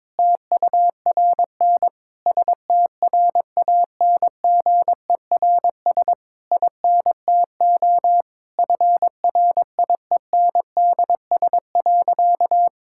Morse code, heard as a keyed tone, about 22 wpm, 700 Hz. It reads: TURN STRANGERS INTO FRIENDS.
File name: Difficult Sentences with Hints - 22wpm